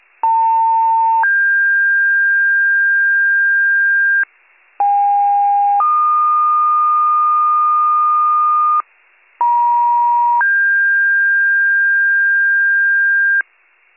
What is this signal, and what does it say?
Two-tone page, heard as a one-second tone, then a three-second tone, about 900 then 1600 Hz, about 810 then 1220 Hz, about 950 then 1600 Hz, with noise. It is heard from 0.2 to 13.4 s.